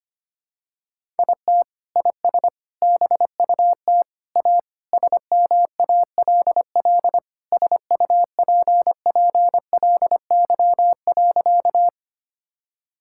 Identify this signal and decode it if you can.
Morse code, heard as a keyed tone, about 25 wpm, 700 Hz. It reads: IT IS BUT A SMALL SUPPLY.
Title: Sentences from Top 1000 Words - Review Part 5 - 25wpm